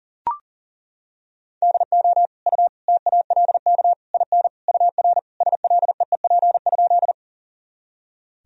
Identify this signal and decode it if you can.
Morse code, heard as a keyed tone, about 37 wpm, 700 Hz. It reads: DO U TALK IN UR SLEEP?
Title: Sentences from Top 500 Words and Abbreviations 40wpm